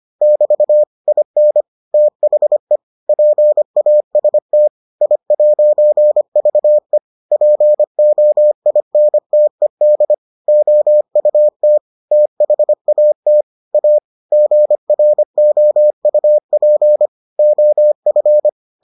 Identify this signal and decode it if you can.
Morse code, heard as a keyed tone, about 25 wpm, 600 Hz. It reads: <BT> IN THE PAST I'VE POINTED OUT THAT A GROUP OF